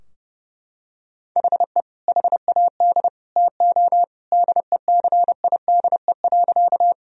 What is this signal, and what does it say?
Morse code, heard as a keyed tone, about 30 wpm, 700 Hz. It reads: HE HAD TO DECIDE.
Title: Sentences from Top 1000 Words - Review Part 3 - 30wpm